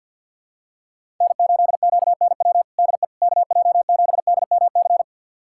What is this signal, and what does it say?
Morse code, heard as a keyed tone, about 51 wpm, 700 Hz. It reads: N8QNW DE KJ7DMC
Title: USA Callsign DE Callsign 50wpm